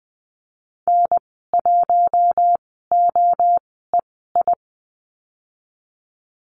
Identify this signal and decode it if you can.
Morse code, heard as a keyed tone, about 20 wpm, spaced as for 14 wpm, 700 Hz. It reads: N1OEI
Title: 042 Lesson 8g - Number 1 - Call Signs 20wpm